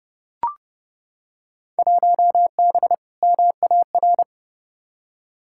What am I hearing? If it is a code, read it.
Morse code, heard as a keyed tone, about 32 wpm, 700 Hz. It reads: 1B MAR
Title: Field Day Exchanges Encore - 30wpm